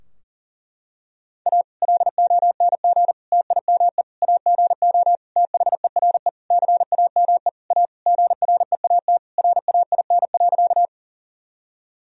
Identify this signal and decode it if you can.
Morse code, heard as a keyed tone, about 37 wpm, 700 Hz. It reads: A LONG TIME AGO THERE CAME A GREAT RAIN.
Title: Sentences from Top 1000 Words - Review Part 3 - 40wpm